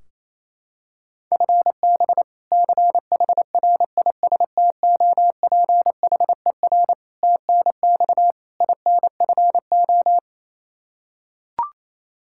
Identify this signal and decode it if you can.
Morse code, heard as a keyed tone, about 28 wpm, 700 Hz. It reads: FB CHRISTOPHER TNX INFO
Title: K1USN SST QSO Practice 28wpm